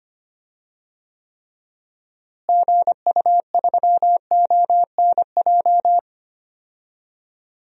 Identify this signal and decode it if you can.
Morse code, heard as a keyed tone, about 25 wpm, 700 Hz. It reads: GU3ONJ
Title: DXCC-100 Call Signs - 6 Characters - Repeated 2x - 25wpm